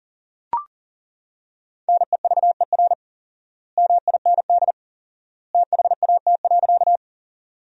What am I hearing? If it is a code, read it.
Morse code, heard as a keyed tone, about 40 wpm, 700 Hz. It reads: NEVER MIND THAT.
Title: 4x Word-Spacing - Sentences from Top 500 Words - 40wpm